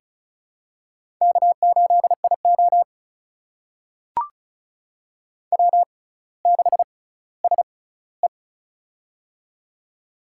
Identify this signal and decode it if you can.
Morse code, heard as a keyed tone, about 36 wpm, spaced as for 18 wpm, 700 Hz. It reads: K8IO W6SE
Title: USA Call Signs - 4 Characters - 6x Character Spacing - 35wpm